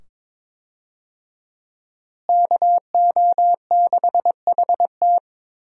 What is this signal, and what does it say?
Morse code, heard as a keyed tone, about 22 wpm, 700 Hz. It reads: KO6HT
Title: USA Call Signs - 5 Characters - Repeated 2x - 22wpm